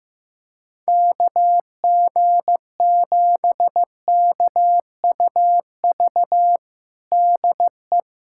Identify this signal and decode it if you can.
Morse code, heard as a keyed tone, about 15 wpm, 700 Hz. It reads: KG7KUV DE
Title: USA Callsign DE Callsign - 6 Characters - 15wpm